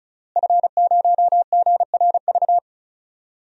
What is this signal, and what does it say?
Morse code, heard as a keyed tone, about 36 wpm, 700 Hz. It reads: F0GRV
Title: DXCC-100 Call Signs - 5 Characters - Repeated 2x - 35wpm